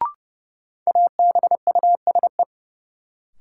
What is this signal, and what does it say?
Morse code, heard as a keyed tone, about 32 wpm, 700 Hz. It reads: ABUSE
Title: Sentences from Top 1300 Words - New Word Drill - 30wpm